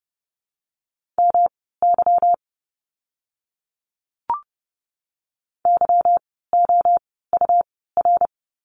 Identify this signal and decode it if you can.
Morse code, heard as a keyed tone, about 30 wpm, spaced as for 17 wpm, 700 Hz. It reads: MY YOUR
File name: course / 30:10wpm / 237 Lesson 38f - Letter q - Words 30wpm